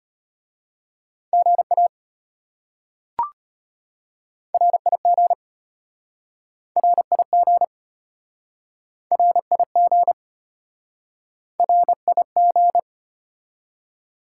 Morse code, heard as a keyed tone, about 32 wpm, 700 Hz. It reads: GA RIG RIG RIG RIG